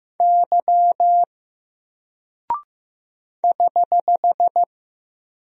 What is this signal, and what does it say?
Morse code, heard as a keyed tone, about 15 wpm, 700 Hz. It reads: Y <HH>
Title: Single Character (Letter:Number:Punctuation:Prosign) 15wpm